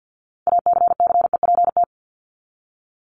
Morse code, heard as a keyed tone, about 50 wpm, 700 Hz. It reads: ACCEPT
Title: ED Suffix - Top 100 Words with Base Words 50wpm